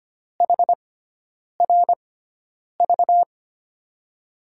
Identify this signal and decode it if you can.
Morse code, heard as a keyed tone, about 25 wpm, spaced as for 9 wpm, 700 Hz. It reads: HRV
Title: USA Call Signs - 6 Characters - 6x Character Spacing - 25wpm